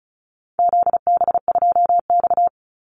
Morse code, heard as a keyed tone, about 34 wpm, 700 Hz. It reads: ZB2X
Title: Call Signs Repeated 2x - 35wpm